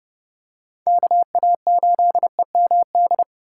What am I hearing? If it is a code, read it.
Morse code, heard as a keyed tone, about 28 wpm, 700 Hz. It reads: KA8EMD